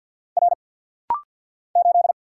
Morse code, heard as a keyed tone, about 50 wpm, 700 Hz. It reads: R 8
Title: Single Letter-Number Alt - Rapid-Fire 50wpm